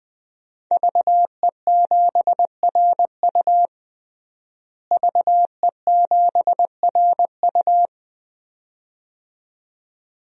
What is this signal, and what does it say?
Morse code, heard as a keyed tone, about 21 wpm, 700 Hz. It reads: VE7RU VE7RU
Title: DXCC-100 Call Signs - 5 Characters - Repeated 2x - 20wpm